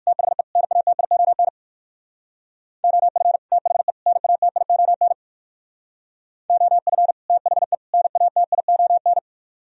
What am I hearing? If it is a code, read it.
Morse code, heard as a keyed tone, about 51 wpm, 700 Hz. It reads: THE NATION OF THE NATION OF THE NATION
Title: Speed-Racing - Sets of 3 Words 45wpm